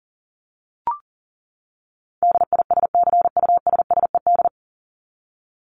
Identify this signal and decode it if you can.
Morse code, heard as a keyed tone, about 40 wpm, 700 Hz. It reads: DISCUSSED
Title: ED Suffix - Top 100 Words with Base Words 40wpm